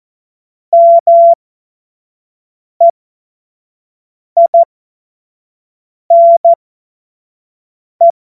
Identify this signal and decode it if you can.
Morse code, heard as a keyed tone, about 14 wpm, spaced as for 5 wpm, 680 Hz. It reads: MEINE
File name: Morsekurs / Witze / Witz 22. Februar 2022 25 F70